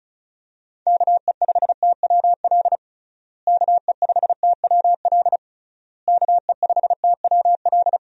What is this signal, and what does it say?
Morse code, heard as a keyed tone, about 34 wpm, 700 Hz. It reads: KE5TWL KE5TWL KE5TWL
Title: USA Call Signs - 6 Characters - Repeated 3x - 35wpm